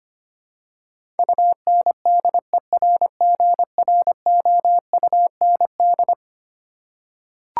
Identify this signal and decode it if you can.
Morse code, heard as a keyed tone, about 25 wpm, 700 Hz. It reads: UNDERGROUND
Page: morse-code-ninja-practice-25wpm